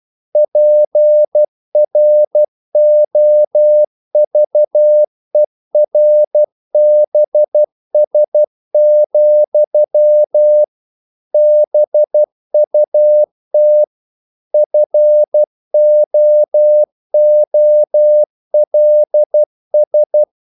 Morse code, heard as a keyed tone, about 12 wpm, 600 Hz. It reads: PROVERBS, BUT FOOLS
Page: English quotations in CW (telegraphy - morse)